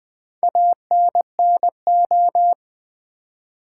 Morse code, heard as a keyed tone, about 20 wpm, 700 Hz. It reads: ANNO
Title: Build A Word - 4 Letters At A Time - 12-Letter Words - 20wpm